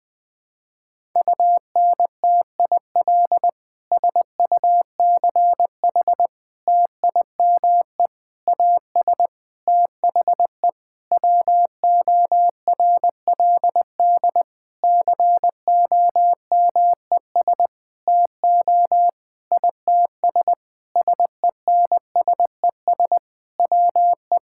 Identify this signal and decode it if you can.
Morse code, heard as a keyed tone, about 20 wpm, 700 Hz. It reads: UNTIL SUCH TIME AS THE WORLD COMES TO ITS SENSES WE
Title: Sentences with 9-Word Phrases - Spoken Beginning - 20wpm